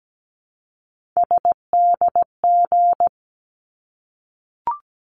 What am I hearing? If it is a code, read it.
Morse code, heard as a keyed tone, about 17 wpm, 700 Hz. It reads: SDG
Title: ARRL Sections 17wpm